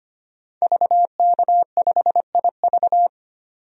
Morse code, heard as a keyed tone, about 25 wpm, 700 Hz. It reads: VK5IV
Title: DXCC-100 Call Signs - 5 Characters - 25wpm